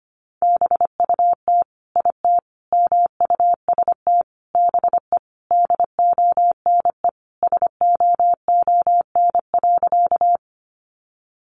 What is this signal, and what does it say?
Morse code, heard as a keyed tone, about 25 wpm, 700 Hz. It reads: BUT IT MUST BE DONE SOON.
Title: Sentences from Top 500 Words 25wpm